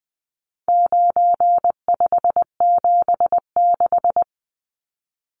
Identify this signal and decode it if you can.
Morse code, heard as a keyed tone, about 20 wpm, 700 Hz. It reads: 9576
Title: Numbers - 4-Digit 20wpm